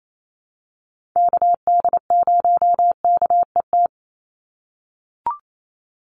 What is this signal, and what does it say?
Morse code, heard as a keyed tone, about 28 wpm, 700 Hz. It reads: KD0KET
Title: USA Call Signs 28wpm